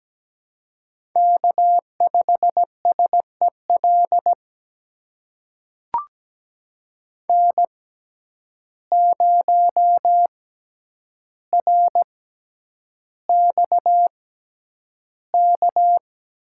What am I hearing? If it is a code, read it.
Morse code, heard as a keyed tone, about 17 wpm, spaced as for 9 wpm, 700 Hz. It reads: K5SEL N0RXK
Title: USA Call Signs - 6x Character Spacing - 17wpm